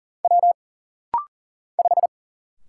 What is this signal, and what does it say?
Morse code, heard as a keyed tone, about 40 wpm, 700 Hz. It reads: W 5
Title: Single Letter-Number Alt - Rapid-Fire 40wpm